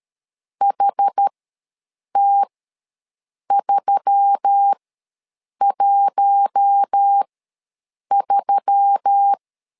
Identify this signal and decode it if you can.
Morse code, heard as a keyed tone, about 13 wpm, spaced as for 7 wpm, 800 Hz. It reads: HT313
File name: LEZIONI / 4 Q1S3 ws 12wpm 2038